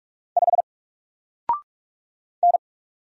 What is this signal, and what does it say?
Morse code, heard as a keyed tone, about 44 wpm, 700 Hz. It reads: 5 N
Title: Single Character (Letter:Number:Punctuation:Prosign) 45wpm